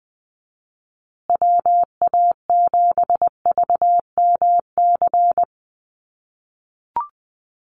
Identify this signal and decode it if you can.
Morse code, heard as a keyed tone, about 20 wpm, 700 Hz. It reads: WA7VMC
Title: USA Call Signs Repeated 3x 20wpm